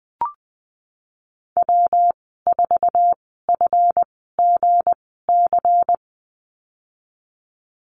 Morse code, heard as a keyed tone, about 20 wpm, spaced as for 15 wpm, 700 Hz. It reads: W4FGC
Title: 179 Lesson 29g - Slash - Call Signs 20wpm